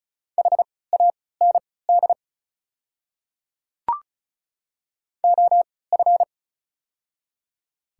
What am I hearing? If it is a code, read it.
Morse code, heard as a keyed tone, about 34 wpm, spaced as for 20 wpm, 690 Hz. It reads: HAND OF
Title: 3x Character Spacing - Top 200 Words - 35wpm